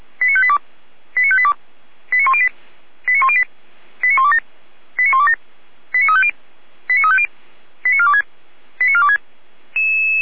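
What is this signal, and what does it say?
Five-tone selective call (ZVEI), heard as quick runs of five tones, tones of about 70 ms. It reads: 89582 89582 89108 89108 89127 89127 89350 89350 89426 89426
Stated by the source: Five-tone ZVEI selective call received on an SDR, channel 455.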